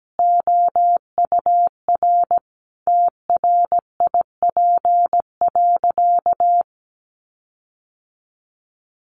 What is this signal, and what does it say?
Morse code, heard as a keyed tone, about 17 wpm, 700 Hz. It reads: OUR TRIP.
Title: Sentences from Top 1300 Words 17wpm